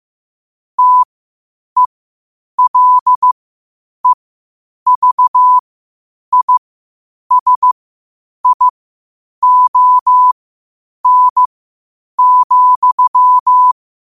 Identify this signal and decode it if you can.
Morse code, heard as a keyed tone, about 15 wpm, spaced as for 8 wpm, 1000 Hz. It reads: TELEVISION,
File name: qotdfiles / 8wpm